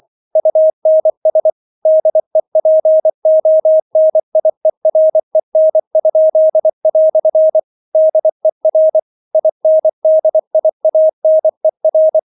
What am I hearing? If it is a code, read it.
Morse code, heard as a keyed tone, about 24 wpm, 630 Hz. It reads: UNS DEPONIEREN?" DER INDIANER